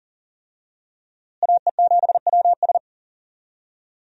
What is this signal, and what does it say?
Morse code, heard as a keyed tone, about 40 wpm, 700 Hz. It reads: AE7WS
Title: USA Call Signs - 5 Characters - Repeated 2x - 40wpm